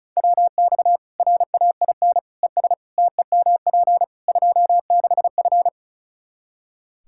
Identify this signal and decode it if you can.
Morse code, heard as a keyed tone, about 34 wpm, 700 Hz. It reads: WX RAIN ES TEMP 26F
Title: Practice QSO - Exchange 2 - Info - 35wpm